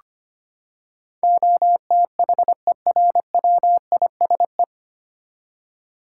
Morse code, heard as a keyed tone, about 25 wpm, 700 Hz. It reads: OTHERWISE
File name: Nine Letter Words - 25wpm